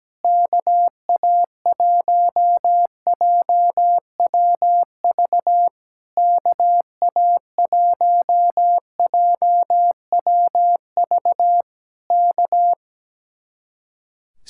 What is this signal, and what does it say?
Morse code, heard as a keyed tone, about 17 wpm, 700 Hz. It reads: KA1JWV KA1JWV K
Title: Speed-Racing Common QSO Phrases - 17wpm